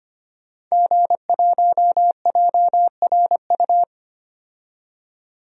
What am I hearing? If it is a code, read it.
Morse code, heard as a keyed tone, about 25 wpm, 700 Hz. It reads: G1JRU
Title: DXCC-100 Call Signs - 5 Characters - 25wpm